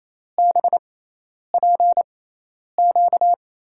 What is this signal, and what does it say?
Morse code, heard as a keyed tone, about 28 wpm, spaced as for 10 wpm, 700 Hz. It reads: BPQ
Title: USA Call Signs - 5 Characters - 6x Character Spacing - 28wpm